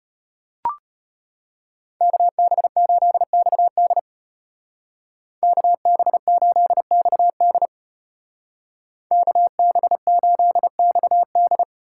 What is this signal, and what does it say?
Morse code, heard as a keyed tone, about 32 wpm, 700 Hz. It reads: KB8XD KB8XD KB8XD